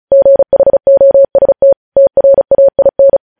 Morse code, heard as a keyed tone, about 36 wpm, 550 Hz. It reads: GHOST TRAIN
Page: News Headlines in Morse Code at 35 WPM